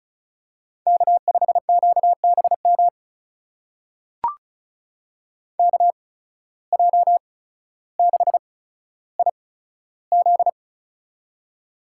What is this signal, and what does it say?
Morse code, heard as a keyed tone, about 36 wpm, spaced as for 15 wpm, 700 Hz. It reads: K5QBM KJ6IZ
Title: USA Call Signs - 5 Characters - 8x Character Spacing - 35wpm